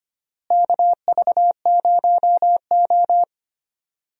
Morse code, heard as a keyed tone, about 25 wpm, 700 Hz. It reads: KV0O
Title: USA Call Signs - 4 Characters - Repeated 2x - 25wpm